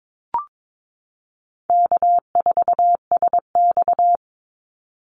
Morse code, heard as a keyed tone, about 22 wpm, 700 Hz. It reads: K4SX
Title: Call Signs Repeated 2x - 22wpm